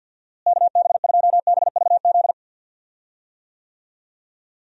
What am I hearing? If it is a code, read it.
Morse code, heard as a keyed tone, about 51 wpm, 700 Hz. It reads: KB2BVZ
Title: USA Call Signs - 6 Characters - Repeated 2x - 50wpm